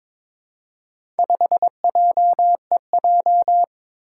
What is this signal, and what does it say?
Morse code, heard as a keyed tone, about 22 wpm, 700 Hz. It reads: 5JEJ
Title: Four Challenging Letter-Number Combinations 22wpm